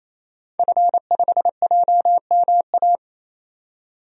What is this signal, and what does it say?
Morse code, heard as a keyed tone, about 28 wpm, 700 Hz. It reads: F5JMA
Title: DXCC-100 Call Signs - 5 Characters - 28wpm